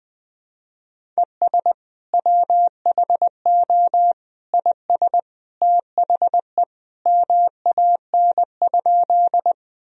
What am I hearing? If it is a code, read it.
Morse code, heard as a keyed tone, about 20 wpm, 700 Hz. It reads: ES WHO IS THE MAN?
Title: Sentences from Top 200 Words and Abbreviations 20wpm